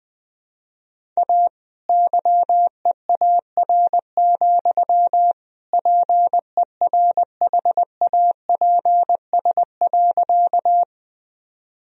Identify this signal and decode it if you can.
Morse code, heard as a keyed tone, about 20 wpm, 700 Hz. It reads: A YEAR, PERHAPS.
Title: Sentences from Top 1000 Words - Review Part 1 - 20wpm